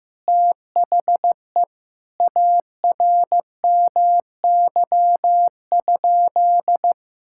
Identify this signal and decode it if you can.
Morse code, heard as a keyed tone, about 15 wpm, 700 Hz. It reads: THE ARMY?
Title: Sentences from Top 1000 Words - Review Part 6 - 15wpm